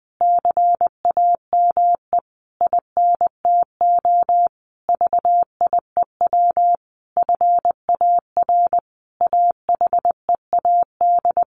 Morse code, heard as a keyed tone, about 20 wpm, 700 Hz. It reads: CAME INTO VIEW FAR AHEAD
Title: Sentences from Top 1100 Words 20wpm